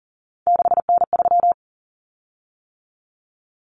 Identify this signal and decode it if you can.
Morse code, heard as a keyed tone, about 39 wpm, 700 Hz. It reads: 6N3
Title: Common Cut Numbers - 3-Digit 40wpm